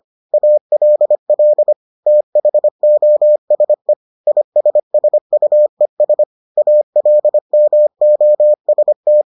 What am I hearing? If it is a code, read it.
Morse code, heard as a keyed tone, about 25 wpm, 600 Hz. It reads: ALL THOSE ISSUES ALMOST